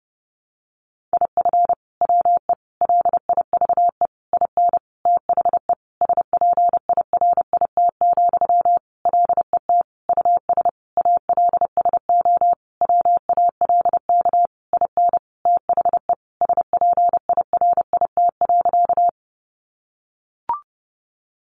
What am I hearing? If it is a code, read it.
Morse code, heard as a keyed tone, about 30 wpm, 700 Hz. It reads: IF WE LIVE IN THE SPIRIT, LET US ALSO WALK IN THE SPIRIT.
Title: Sentences from Top 1400 Words 30wpm